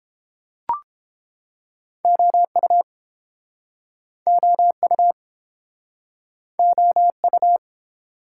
Morse code, heard as a keyed tone, about 33 wpm, 700 Hz. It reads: OU OU OU